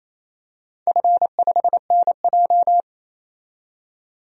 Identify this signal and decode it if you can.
Morse code, heard as a keyed tone, about 28 wpm, 700 Hz. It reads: F5NJ
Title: DXCC-100 Call Signs - 4 Characters - 28wpm